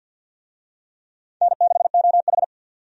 Morse code, heard as a keyed tone, about 48 wpm, 700 Hz. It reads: N6OH
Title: USA Call Signs - 4 Characters - 50wpm